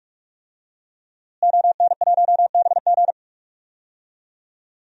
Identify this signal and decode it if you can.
Morse code, heard as a keyed tone, about 46 wpm, 700 Hz. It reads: ON1BG
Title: DXCC-100 Call Signs - 5 Characters - Repeated 3x - 45wpm